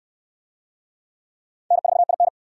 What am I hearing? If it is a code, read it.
Morse code, heard as a keyed tone, about 68 wpm, 700 Hz. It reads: N4IN